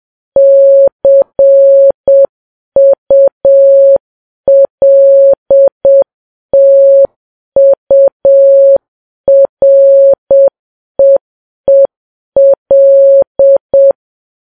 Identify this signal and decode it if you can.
Morse code, heard as a keyed tone, about 7 wpm, 550 Hz. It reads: CULTUREEL
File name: Files Web tls / Morse on9fv / morse7kps